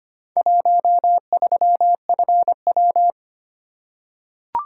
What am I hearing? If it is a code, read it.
Morse code, heard as a keyed tone, about 25 wpm, 700 Hz. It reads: 13FW
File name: practice / 25wpm / Four Letter-Number Combinations - 25wpm